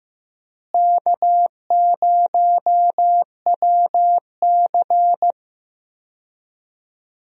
Morse code, heard as a keyed tone, about 15 wpm, 700 Hz. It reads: K0WC
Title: USA Call Signs - 4 Characters - Repeated 3x - 15wpm